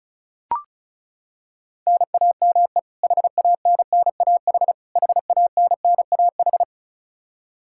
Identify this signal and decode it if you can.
Morse code, heard as a keyed tone, about 36 wpm, 700 Hz. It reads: NAME HANNAH HANNAH
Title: Practice QSO - Exchange 1 - Beginning - 35wpm